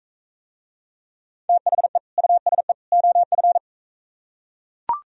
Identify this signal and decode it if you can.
Morse code, heard as a keyed tone, about 43 wpm, 700 Hz. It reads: THE USE OF